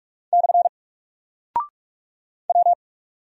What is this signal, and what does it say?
Morse code, heard as a keyed tone, about 46 wpm, 700 Hz. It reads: / W
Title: Single Character (Letter:Number:Punctuation:Prosign) Alt 45wpm